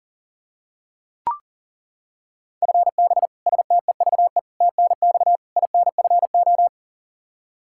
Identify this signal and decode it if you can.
Morse code, heard as a keyed tone, about 40 wpm, 700 Hz. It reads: FB STEVE TNX INFO
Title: Practice QSO - Exchange 3 - More Info - 40wpm